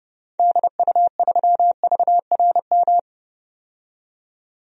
Morse code, heard as a keyed tone, about 31 wpm, 700 Hz. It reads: DU3VRM
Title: DXCC-100 Call Signs - 6 Characters - 30wpm